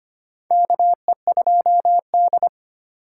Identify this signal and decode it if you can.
Morse code, heard as a keyed tone, about 25 wpm, 700 Hz. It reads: KE2D
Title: K1USN SST QSO Practice - 25wpm